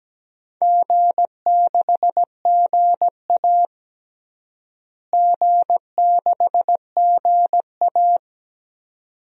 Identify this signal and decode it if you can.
Morse code, heard as a keyed tone, about 17 wpm, 700 Hz. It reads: G6GA G6GA
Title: DXCC-100 Call Signs - 4 Characters - Repeated 2x - 17wpm